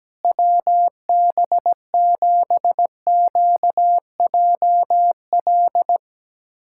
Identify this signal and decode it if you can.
Morse code, heard as a keyed tone, about 17 wpm, 700 Hz. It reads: WB7QJL
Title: USA Callsign DE Callsign - 6 Characters - 17wpm